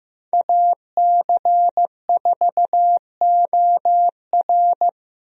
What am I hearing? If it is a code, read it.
Morse code, heard as a keyed tone, about 15 wpm, 700 Hz. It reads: AC4OR
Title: USA Call Signs - 5 Characters - Repeated 3x - 15wpm